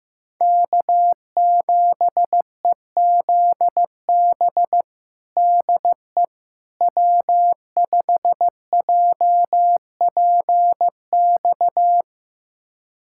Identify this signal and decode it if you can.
Morse code, heard as a keyed tone, about 15 wpm, 700 Hz. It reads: K7EZB DE W5JPX
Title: USA Callsign DE Callsign - 5 Characters - 15wpm